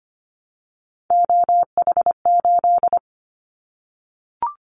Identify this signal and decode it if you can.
Morse code, heard as a keyed tone, about 25 wpm, 700 Hz. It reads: OH8
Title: Callsign Prefixes for Top 25 DXCC - Repeated 3x - 25wpm